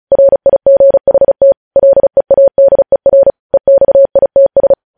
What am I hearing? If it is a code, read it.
Morse code, heard as a keyed tone, about 36 wpm, 550 Hz. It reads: RIGHT LEADER EXITS
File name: podcast / News headlines-1x-35x35